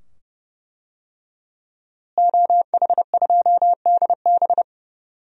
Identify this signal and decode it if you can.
Morse code, heard as a keyed tone, about 31 wpm, 700 Hz. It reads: OH2DB
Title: DXCC-100 Call Signs - 5 Characters - Repeated 2x - 30wpm